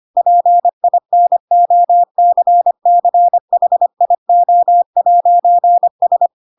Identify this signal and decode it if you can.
Morse code, heard as a keyed tone, about 25 wpm, 700 Hz. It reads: PINOCCHIO'S
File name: Pinocchio / Pinocchio ch3 25 25 ik1hge